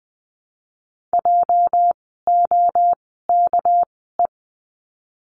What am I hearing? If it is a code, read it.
Morse code, heard as a keyed tone, about 20 wpm, spaced as for 14 wpm, 700 Hz. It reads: JOKE
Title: 206 Lesson 33f - Letter j - Words 20wpm